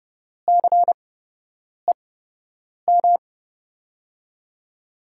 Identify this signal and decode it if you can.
Morse code, heard as a keyed tone, about 30 wpm, spaced as for 8 wpm, 700 Hz. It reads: CEM
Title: USA Call Signs - 5 Characters - 8x Character Spacing - 30wpm